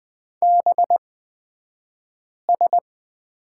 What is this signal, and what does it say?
Morse code, heard as a keyed tone, about 20 wpm, spaced as for 5 wpm, 700 Hz. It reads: BS
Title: 223 Lesson 36d - BT - Two Characters 20wpm